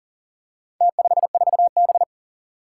Morse code, heard as a keyed tone, about 40 wpm, 700 Hz. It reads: T54B
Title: Four Challenging Letter-Number Combinations 40wpm